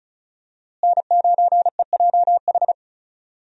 Morse code, heard as a keyed tone, about 36 wpm, 700 Hz. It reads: N9EJH
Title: USA Call Signs - 5 Characters - Repeated 3x - 35wpm